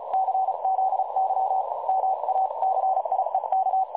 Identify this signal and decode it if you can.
Morse code, heard as a keyed tone, about 32 wpm, 760 Hz, with noise. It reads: CZY MACIEK
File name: dlazaawansowanych / nagraniazpasma / websdr recording 3565.0kHz